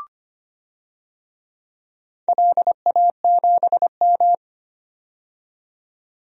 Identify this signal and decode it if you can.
Morse code, heard as a keyed tone, about 25 wpm, 700 Hz. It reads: LA7M
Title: DXCC-100 Call Signs - 4 Characters - 25wpm